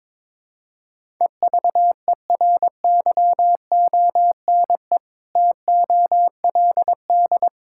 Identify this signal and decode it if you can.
Morse code, heard as a keyed tone, about 22 wpm, 700 Hz. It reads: EVERYONE TOLD
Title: Sentences from Top 1000 Words - Review Part 5 - 22wpm